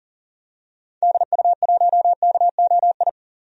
Morse code, heard as a keyed tone, about 39 wpm, 700 Hz. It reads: DU1KOI
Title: DXCC-100 Call Signs - 6 Characters - 40wpm